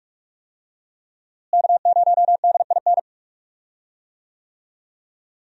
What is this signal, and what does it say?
Morse code, heard as a keyed tone, about 41 wpm, 700 Hz. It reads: K0DIN
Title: USA Call Signs - 5 Characters - Repeated 3x - 45wpm